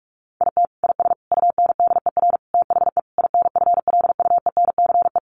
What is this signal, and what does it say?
Morse code, heard as a keyed tone, about 44 wpm, 700 Hz. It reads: IT IS UNDER THE INFLUENCE
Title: Sentences from Top 1400 Words 45wpm